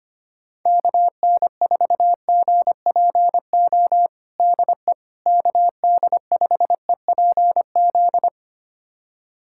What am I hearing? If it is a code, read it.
Morse code, heard as a keyed tone, about 25 wpm, 700 Hz. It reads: KN4GPO DE KD5EPZ
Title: USA Callsign DE Callsign - 6 Characters - 25wpm